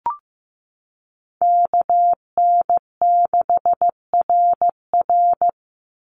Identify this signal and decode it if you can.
Morse code, heard as a keyed tone, about 15 wpm, 700 Hz. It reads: KN6RR
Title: USA Call Signs 15wpm